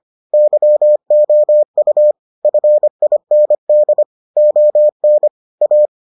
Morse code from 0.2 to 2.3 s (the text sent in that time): YOU